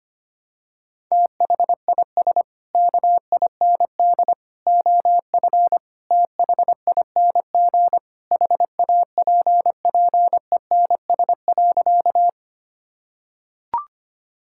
Morse code, heard as a keyed tone, about 25 wpm, 700 Hz. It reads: THIS KIND OF THING HAPPENS.
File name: practice / 25wpm / Sentences with 3-Word Phrases - 25wpm